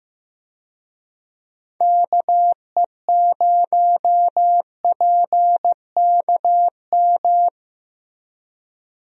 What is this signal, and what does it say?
Morse code, heard as a keyed tone, about 15 wpm, 700 Hz. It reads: KE0PKM
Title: USA Call Signs - 6 Characters - Repeated 3x - 15wpm